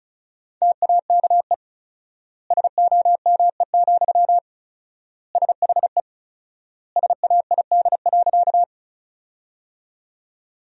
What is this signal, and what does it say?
Morse code, heard as a keyed tone, about 34 wpm, 700 Hz. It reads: TAKE SOME, SHE SAID.
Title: 4x Word-Spacing - Sentences from Top 200 Words - 35wpm